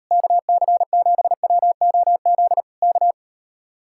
Morse code, heard as a keyed tone, about 38 wpm, 700 Hz. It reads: KC7WOZ K